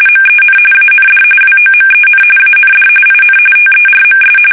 RTTY, about 45.45 baud. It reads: MEN AND WOMEN AND IN TERMS
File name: fsk850